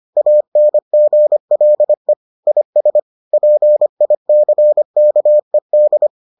Morse code, heard as a keyed tone, about 25 wpm, 600 Hz. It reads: ANGLE IS PICKED